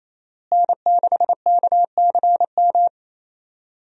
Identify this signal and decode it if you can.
Morse code, heard as a keyed tone, about 28 wpm, 700 Hz. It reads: N6KCM
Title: USA Call Signs - 5 Characters - Repeated 3x - 28wpm